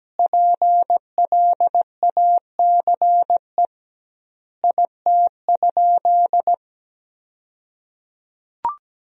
Morse code, heard as a keyed tone, about 17 wpm, 700 Hz. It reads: PLACE IT?
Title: 2x Word-Spacing - Sentences from Top 200 Words - 17wpm